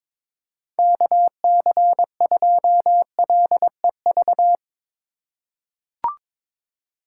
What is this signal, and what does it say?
Morse code, heard as a keyed tone, about 22 wpm, 700 Hz. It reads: KC2LEV
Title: USA Call Signs - 4x Character Spacing - 22wpm